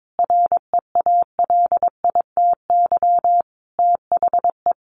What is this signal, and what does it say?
Morse code, heard as a keyed tone, about 22 wpm, 700 Hz. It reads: REALITY THE
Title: Sentences from Top 1200 Words 22wpm